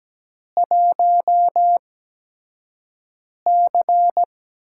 Morse code, heard as a keyed tone, about 17 wpm, spaced as for 5 wpm, 700 Hz. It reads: 1C
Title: USA Call Signs - 8x Character Spacing - 17wpm